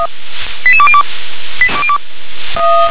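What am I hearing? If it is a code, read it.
Five-tone selective call (ZVEI), heard as quick runs of five tones, tones of about 70 ms. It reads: 88202
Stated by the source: Five-tone ZVEI selective call, received by SDR.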